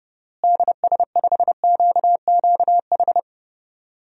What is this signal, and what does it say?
Morse code, heard as a keyed tone, about 30 wpm, 700 Hz. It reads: DS5QQH
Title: DXCC-100 Call Signs - 6 Characters - Repeated 3x - 30wpm